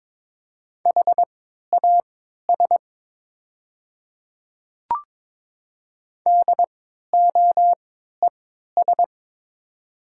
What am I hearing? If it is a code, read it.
Morse code, heard as a keyed tone, about 23 wpm, spaced as for 13 wpm, 700 Hz. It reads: HAS DOES
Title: 3x Character Spacing - Top 200 Words - 22wpm